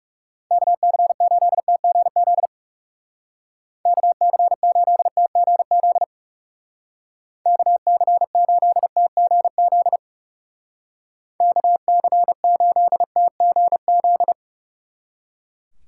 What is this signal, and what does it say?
Morse code, heard as a keyed tone, about 36 wpm, 700 Hz. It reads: KC8TGZ KC8TGZ KC8TGZ KC8TGZ
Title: Speed Racing - USA 2x3 Call Signs 30wpm